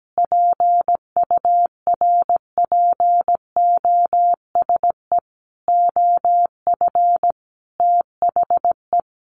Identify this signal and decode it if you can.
Morse code, heard as a keyed tone, about 17 wpm, 700 Hz. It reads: PURPOSE OF THE
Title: Sentences from Top 1400 Words 17wpm